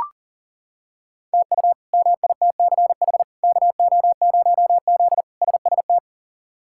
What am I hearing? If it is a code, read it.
Morse code, heard as a keyed tone, about 40 wpm, 700 Hz. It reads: TU MITCH KO0Z SST
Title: K1USN SST QSO Practice - 40wpm